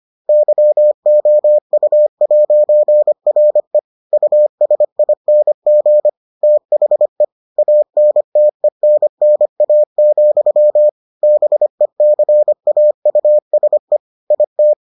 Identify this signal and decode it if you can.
Morse code, heard as a keyed tone, about 25 wpm, 600 Hz. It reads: YOU'RE USING THE ANTENNA, BECAUSE IT